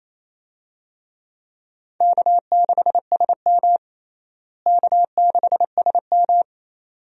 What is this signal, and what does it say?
Morse code, heard as a keyed tone, about 27 wpm, 700 Hz. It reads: K6SM K6SM
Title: USA Call Signs - 4 Characters - Repeated 2x - 28wpm